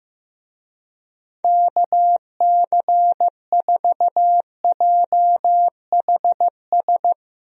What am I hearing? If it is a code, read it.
Morse code, heard as a keyed tone, about 15 wpm, 700 Hz. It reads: KC4JHS
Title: USA Call Signs - 6 Characters - 15wpm